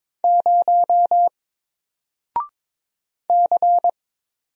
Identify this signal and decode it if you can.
Morse code, heard as a keyed tone, about 22 wpm, 700 Hz. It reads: 0 C
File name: Single-Letter-Number-22wpm